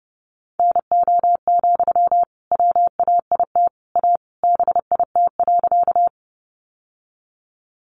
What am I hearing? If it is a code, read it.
Morse code, heard as a keyed tone, about 30 wpm, 700 Hz. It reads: NO, WAIT A BIT.
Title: Sentences from Top 900 Words 30wpm